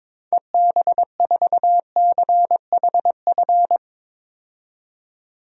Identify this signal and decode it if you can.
Morse code, heard as a keyed tone, about 21 wpm, 700 Hz. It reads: EB4CHF
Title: DXCC-100 Call Signs - 6 Characters - 22wpm